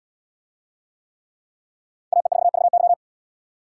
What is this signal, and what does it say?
Morse code, heard as a keyed tone, about 72 wpm, 700 Hz. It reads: AE4VY